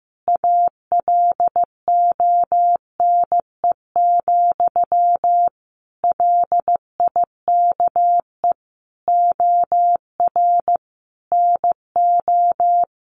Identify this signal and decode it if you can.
Morse code, heard as a keyed tone, about 15 wpm, 700 Hz. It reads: ALONE, LIKE OR NO
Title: Sentences from Top 1000 Words 15wpm